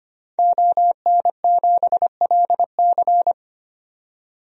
Morse code, heard as a keyed tone, about 25 wpm, 700 Hz. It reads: ON7LC
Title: DXCC-100 Call Signs - 5 Characters - 25wpm